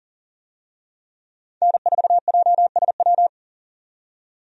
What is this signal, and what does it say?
Morse code, heard as a keyed tone, about 40 wpm, 700 Hz. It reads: N4JSW
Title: USA Call Signs - 5 Characters - 40wpm